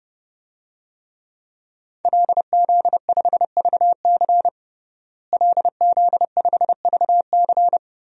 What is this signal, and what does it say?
Morse code, heard as a keyed tone, about 30 wpm, 700 Hz. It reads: LZ5VC LZ5VC
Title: DXCC-100 Call Signs - 5 Characters - Repeated 3x - 30wpm